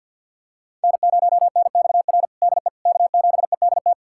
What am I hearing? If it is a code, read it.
Morse code, heard as a keyed tone, about 48 wpm, 700 Hz. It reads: N0NXR DE K7EDT
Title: USA Callsign DE Callsign - 5 Characters - 50wpm